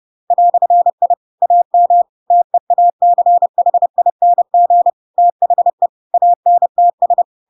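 Morse code, heard as a keyed tone, about 30 wpm, 700 Hz. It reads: "I AM TEACHING THE ANTS
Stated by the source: Chapter 2 of The Adventures of Pinocchio read as a book in Morse code (CW) at 30 wpm, by ik1hge.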